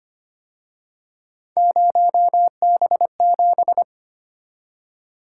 Morse code, heard as a keyed tone, about 30 wpm, 700 Hz. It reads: 0B7
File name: Speed-Racing - Three Letter-Number Combinations - 25wpm